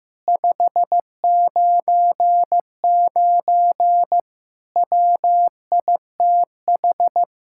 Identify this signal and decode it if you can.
Morse code, heard as a keyed tone, about 15 wpm, 700 Hz. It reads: 599 WITH
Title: Speed-Racing Common QSO Phrases Encore - 15wpm